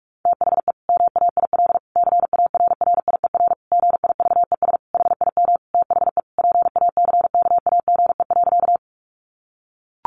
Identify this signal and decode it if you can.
Morse code, heard as a keyed tone, about 44 wpm, 700 Hz. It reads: THE MAIL CARRIER GIVES HIM THE PACKAGE.
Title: Additional Easy Phrases 45wpm